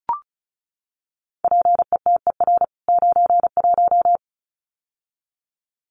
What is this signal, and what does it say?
Morse code, heard as a keyed tone, about 34 wpm, 700 Hz. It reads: PETER 91
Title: CWT Exchange 35wpm